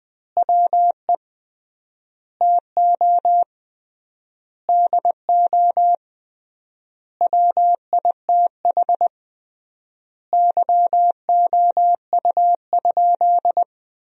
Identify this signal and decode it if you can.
Morse code, heard as a keyed tone, about 20 wpm, 700 Hz. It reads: WE TO DO WITH YOU?
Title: 3x Word-Spacing - Sentences from Top 100 Words - 20wpm